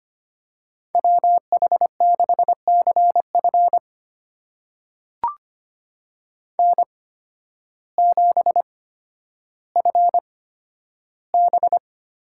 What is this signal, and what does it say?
Morse code, heard as a keyed tone, about 25 wpm, spaced as for 12 wpm, 700 Hz. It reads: WH6CF N7FB
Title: USA Call Signs - 8x Character Spacing - 25wpm